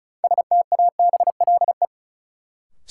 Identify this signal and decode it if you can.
Morse code, heard as a keyed tone, about 36 wpm, 700 Hz. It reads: STABLE
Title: Sentences from Top 1500 Words - New Word Drill Alt - 35wpm